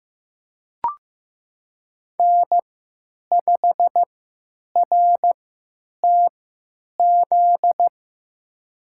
Morse code, heard as a keyed tone, about 15 wpm, spaced as for 9 wpm, 700 Hz. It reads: N5RTZ
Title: USA Call Signs - 5 Characters - 3x Character Spacing - 15wpm